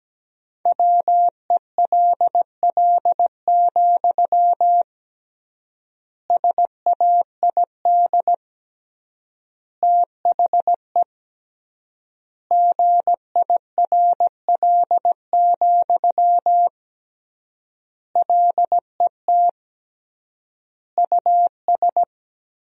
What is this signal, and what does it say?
Morse code, heard as a keyed tone, about 17 wpm, 700 Hz. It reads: WELL, SAID THE GIRL, LET US
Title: 3x Word-Spacing - Sentences from Top 400 Words - 17wpm